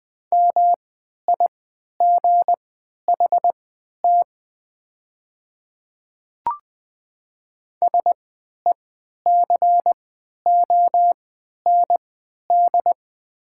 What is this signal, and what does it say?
Morse code, heard as a keyed tone, about 20 wpm, spaced as for 11 wpm, 690 Hz. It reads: MIGHT SECOND
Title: 3x Character Spacing - Top 300 Words - 20wpm